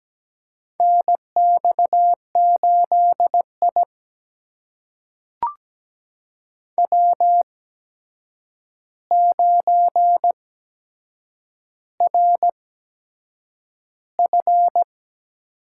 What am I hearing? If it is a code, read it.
Morse code, heard as a keyed tone, about 17 wpm, spaced as for 7 wpm, 700 Hz. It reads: NX8I W9RF
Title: USA Call Signs - 4 Characters - 8x Character Spacing - 17wpm